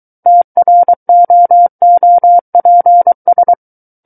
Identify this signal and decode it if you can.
Morse code, heard as a keyed tone, about 23 wpm, 700 Hz. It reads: TROOPS
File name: Book One / 14-Chapter 14 - 23 WPM